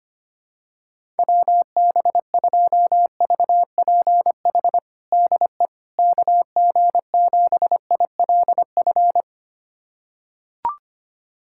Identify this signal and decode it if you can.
Morse code, heard as a keyed tone, about 25 wpm, 700 Hz. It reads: WB2VPH DE KG7ILF
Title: USA Callsign DE Callsign 25wpm